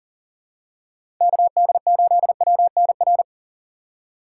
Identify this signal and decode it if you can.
Morse code, heard as a keyed tone, about 40 wpm, 700 Hz. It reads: KD8WNR
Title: USA Call Signs - 6x Character Spacing - 40wpm